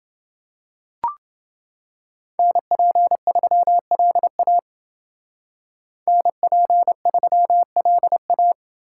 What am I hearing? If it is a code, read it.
Morse code, heard as a keyed tone, about 28 wpm, 700 Hz. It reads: NP3LA NP3LA